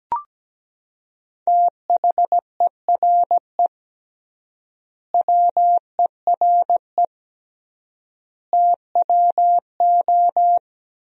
Morse code, heard as a keyed tone, about 17 wpm, 700 Hz. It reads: THERE WERE TWO
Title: 3x Word-Spacing - Sentences from Top 100 Words - 17wpm